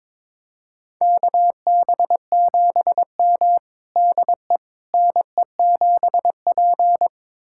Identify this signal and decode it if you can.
Morse code, heard as a keyed tone, about 22 wpm, 700 Hz. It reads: KB7M DE NE7P
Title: USA Callsign DE Callsign - 4 Characters - 22wpm